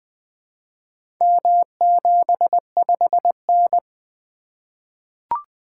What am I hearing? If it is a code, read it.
Morse code, heard as a keyed tone, about 20 wpm, 700 Hz. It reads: M75N
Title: Four Letter-Number Combinations - 20wpm